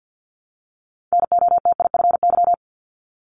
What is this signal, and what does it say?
Morse code, heard as a keyed tone, about 50 wpm, 700 Hz. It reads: NOTIFY
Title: Multi-letter Phonemes 50wpm